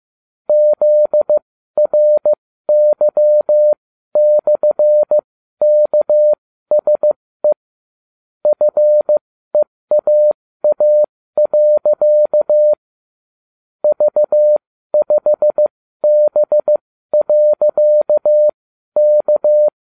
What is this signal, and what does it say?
Morse code, heard as a keyed tone, about 15 wpm, spaced as for 12 wpm, 600 Hz. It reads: ZRY/KSE FEAA. V5B.K